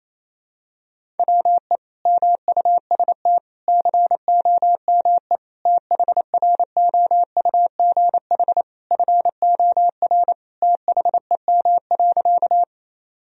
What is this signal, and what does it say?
Morse code, heard as a keyed tone, about 28 wpm, 700 Hz. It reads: WE MUST COME THROUGH FOR THEM.
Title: Sentences from Top 1000 Words - Review Part 1 - 28wpm